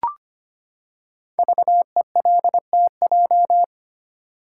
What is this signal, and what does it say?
Morse code, heard as a keyed tone, about 25 wpm, 700 Hz. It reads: VELTJ
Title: Code Groups - 5 Letter:Numbers - 25wpm